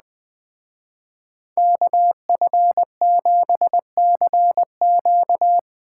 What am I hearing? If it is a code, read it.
Morse code, heard as a keyed tone, about 20 wpm, 700 Hz. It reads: KF7CQ
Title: Speed Racing - USA Call Signs 17wpm